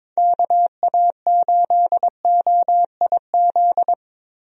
Morse code, heard as a keyed tone, about 22 wpm, 700 Hz. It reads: KA8OIZ
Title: USA Call Signs - 6 Characters - Repeated 3x - 22wpm